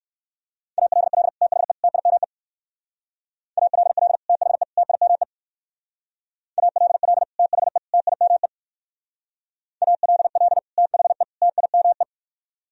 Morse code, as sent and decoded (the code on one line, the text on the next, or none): .- .-.. .-.. / - .... / - .. -- . / .- .-.. .-.. / - .... / - .. -- . / .- .-.. .-.. / - .... / - .. -- . / .- .-.. .-.. / - .... / - .. -- .
ALL THE TIME ALL THE TIME ALL THE TIME ALL THE TIME